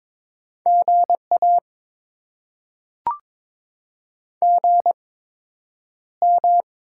Morse code, heard as a keyed tone, about 22 wpm, spaced as for 9 wpm, 700 Hz. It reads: GA GM
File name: Top 20 QSO Elements - 8x Character Spacing - 22wpm